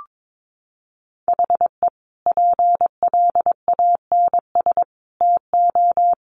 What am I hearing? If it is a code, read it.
Morse code, heard as a keyed tone, about 22 wpm, 700 Hz. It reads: HE PLANS TO
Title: Additional Easy Phrases 22wpm